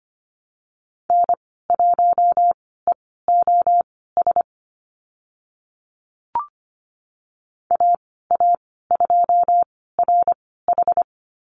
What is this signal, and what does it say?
Morse code, heard as a keyed tone, about 25 wpm, spaced as for 16 wpm, 700 Hz. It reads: N1EOS AA2RH
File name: Lesson 15g - Number 5 - Call Signs 25wpm